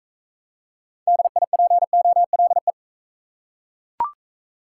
Morse code, heard as a keyed tone, about 40 wpm, 700 Hz. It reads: DIPOLE